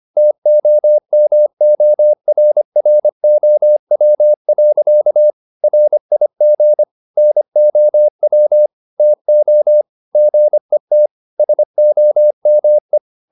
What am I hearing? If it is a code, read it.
Morse code, heard as a keyed tone, about 25 wpm, 600 Hz. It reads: TOMORROW. RIG NOW TO GET SOME